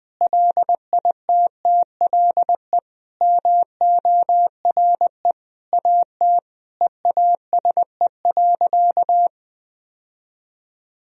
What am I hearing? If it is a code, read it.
Morse code, heard as a keyed tone, about 19 wpm, 700 Hz. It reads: LITTLE MORE AT EASE.
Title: Sentences from Top 1000 Words - Review Part 1 - 20wpm